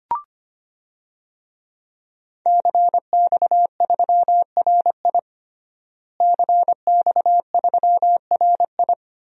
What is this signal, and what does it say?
Morse code, heard as a keyed tone, about 25 wpm, 700 Hz. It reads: CX3RI CX3RI
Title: DXCC-100 Call Signs - 5 Characters - Repeated 3x - 25wpm